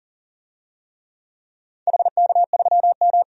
Morse code, heard as a keyed tone, about 40 wpm, 700 Hz. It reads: HK3M
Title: DXCC-100 Call Signs - 4 Characters - 40wpm